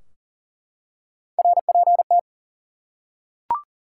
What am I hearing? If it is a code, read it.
Morse code, heard as a keyed tone, about 40 wpm, 700 Hz. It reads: RPT
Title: Top 180 QSO Elements 40wpm